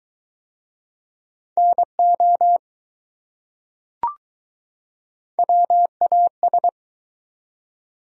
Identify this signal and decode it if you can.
Morse code, heard as a keyed tone, about 21 wpm, 700 Hz. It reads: NO WAS